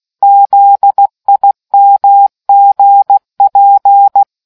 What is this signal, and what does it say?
Morse code, heard as a keyed tone, about 16 wpm, 800 Hz. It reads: ZIMGP